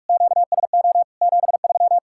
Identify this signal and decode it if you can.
Morse code, heard as a keyed tone, about 56 wpm, 700 Hz. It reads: QSO 73
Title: Speed-Racing Common QSO Phrases Encore - 45wpm